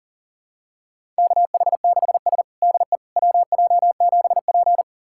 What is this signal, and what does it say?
Morse code, heard as a keyed tone, about 40 wpm, 700 Hz. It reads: KH6S DE WJ7P
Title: USA Callsign DE Callsign - 4 Characters - 40wpm